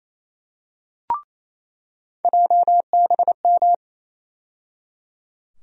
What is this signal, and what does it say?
Morse code, heard as a keyed tone, about 28 wpm, 700 Hz. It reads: JBM
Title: USA Call Sign Postfixes - 3 Letters 28wpm